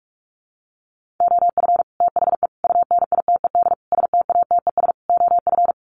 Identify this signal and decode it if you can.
Morse code, heard as a keyed tone, about 44 wpm, 700 Hz. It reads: OF THE UNITED STATES OF
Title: Sets of 5 Words Encore 45wpm